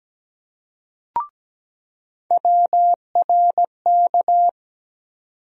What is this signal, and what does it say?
Morse code, heard as a keyed tone, about 17 wpm, 700 Hz. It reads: WRK
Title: USA Call Sign Postfixes - 3 Letters 17wpm